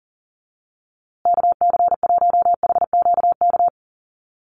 Morse code, heard as a keyed tone, about 40 wpm, 700 Hz. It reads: KC1HQK
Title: USA Call Signs Repeated 3x 40wpm